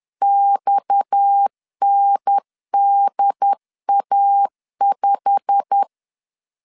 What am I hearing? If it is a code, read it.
Morse code, heard as a keyed tone, about 10 wpm, 800 Hz. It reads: XNDA5
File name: LEZIONI / 10 SIGNS / 10C DOT 10wpm 2940